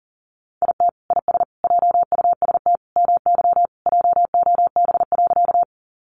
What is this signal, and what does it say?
Morse code, heard as a keyed tone, about 40 wpm, 700 Hz. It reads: IT IS JUST MY JOB.
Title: Sentences from Top 600 Words 40wpm